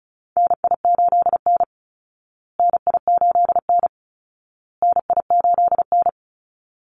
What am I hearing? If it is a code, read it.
Morse code, heard as a keyed tone, about 34 wpm, 700 Hz. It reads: NI8N NI8N NI8N
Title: Call Signs Repeated 2x - 35wpm